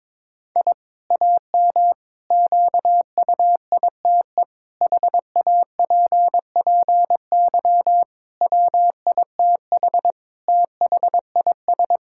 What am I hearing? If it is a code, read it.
Morse code, heard as a keyed tone, about 22 wpm, 700 Hz. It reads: I AM QUITE HAPPY WITH THIS
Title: Sentences from Top 1000 Words - Review Part 5 - 22wpm